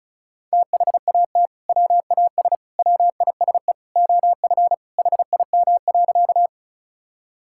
Morse code, heard as a keyed tone, about 36 wpm, 700 Hz. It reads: THAT WAS WISE OF HIM.
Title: Sentences from Top 1600 Words 35wpm